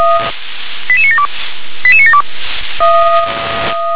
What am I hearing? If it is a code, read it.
Five-tone selective call (ZVEI), heard as quick runs of five tones, tones of about 80 ms. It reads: 88972 88972
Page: /357/SDR/ZVEI/88150/2021.09.04/